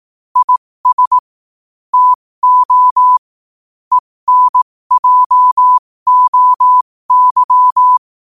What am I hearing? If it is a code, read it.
Morse code, heard as a keyed tone, about 18 wpm, spaced as for 15 wpm, 1000 Hz. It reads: IS TO ENJOY